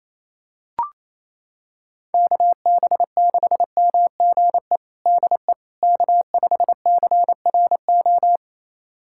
Morse code, heard as a keyed tone, about 28 wpm, 700 Hz. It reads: KB6MGE DE K5CRO
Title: USA Callsign DE Callsign 28wpm